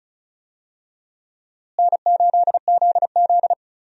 Morse code, heard as a keyed tone, about 36 wpm, 700 Hz. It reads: N8ZZ
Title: USA Call Signs - 4 Characters - Repeated 2x - 35wpm